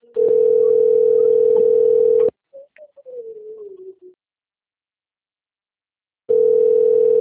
Ringback tone (North America), 440 + 480 Hz.